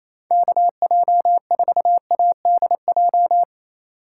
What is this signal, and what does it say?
Morse code, heard as a keyed tone, about 28 wpm, 700 Hz. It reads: KJ4ADJ
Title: USA Call Signs - 6 Characters - 2x Character Spacing - 28wpm